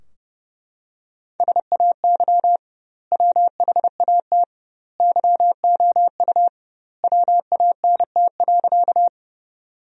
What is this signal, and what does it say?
Morse code, heard as a keyed tone, about 30 wpm, 700 Hz. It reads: SAY WHAT YOU WANT.
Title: 2x Word-Spacing - Sentences from Top 200 Words - 30wpm